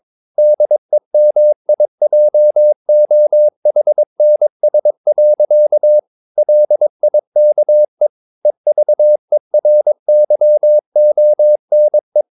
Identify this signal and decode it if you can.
Morse code, heard as a keyed tone, about 22 wpm, 600 Hz. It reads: DEMIJOHNS. LIKE EVERYONE
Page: Romanen Never af Ken Follet på engelsk.